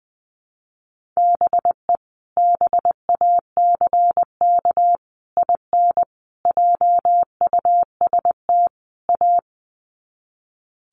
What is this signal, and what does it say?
Morse code, heard as a keyed tone, about 20 wpm, 700 Hz. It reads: BE BACK IN JUST A
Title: Sets of 5 Words Encore 20wpm